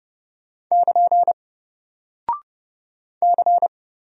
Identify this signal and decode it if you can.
Morse code, heard as a keyed tone, about 30 wpm, 700 Hz. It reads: <KN> C
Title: Single Character (Letter:Number:Punctuation:Prosign) Alt 30wpm